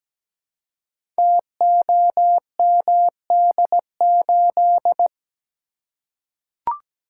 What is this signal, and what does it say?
Morse code, heard as a keyed tone, about 17 wpm, 700 Hz. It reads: TOMD8
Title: Code Groups - 5 Letter:Numbers - 17wpm